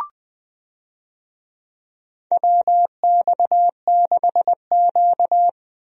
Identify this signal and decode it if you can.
Morse code, heard as a keyed tone, about 20 wpm, 700 Hz. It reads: WX6Q
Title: USA Call Signs - 4 Characters - Repeated 2x - 20wpm